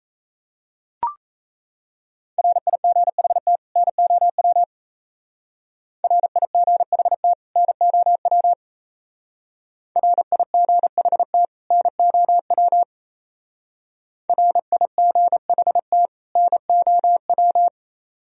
Morse code, heard as a keyed tone, about 35 wpm, 700 Hz. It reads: RIGHT NOW RIGHT NOW RIGHT NOW RIGHT NOW
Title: Speed-Racing - Sets of 2 Words 28wpm